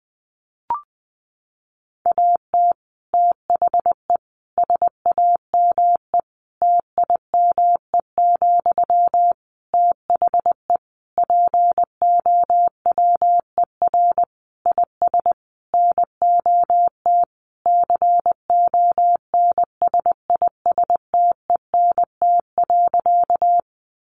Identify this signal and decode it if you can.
Morse code, heard as a keyed tone, about 20 wpm, 700 Hz. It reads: AT THE SAME TIME, THE POWER IS NOT CONSISTENT.
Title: Sentences from Top 1500 Words 20wpm